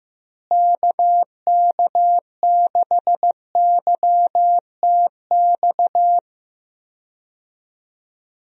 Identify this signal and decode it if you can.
Morse code, heard as a keyed tone, about 15 wpm, 700 Hz. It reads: KK6YTX
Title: USA Call Signs - 6 Characters - Repeated 2x - 15wpm